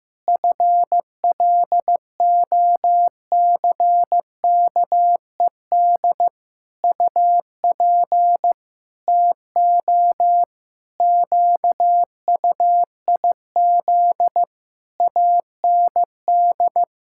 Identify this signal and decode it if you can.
Morse code, heard as a keyed tone, about 15 wpm, 700 Hz. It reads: FLOCKED UP TO QUIZ AND
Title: Pangram Challenge v2 - 15wpm